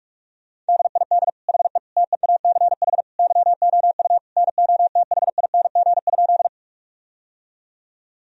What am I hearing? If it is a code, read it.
Morse code, heard as a keyed tone, about 46 wpm, 700 Hz. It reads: DID HE TEACH YOU NOTHING?
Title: Sentences from Top 1000 Words - Review Part 2 - 45wpm